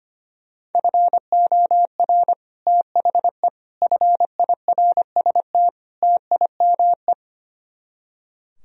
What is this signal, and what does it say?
Morse code, heard as a keyed tone, about 30 wpm, 700 Hz. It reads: FOR THE FIRST TIME
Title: Speed-Racing - Sets of 4 Words 25wpm